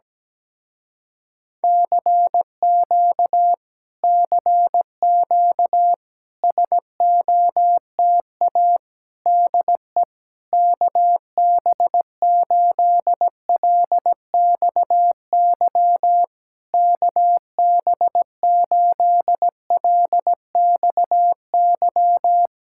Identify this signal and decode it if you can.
Morse code, heard as a keyed tone, about 17 wpm, 700 Hz. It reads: CQ CQ SOTA DE KB8LXY KB8LXY
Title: Speed-Racing Common QSO Phrases Encore - 17wpm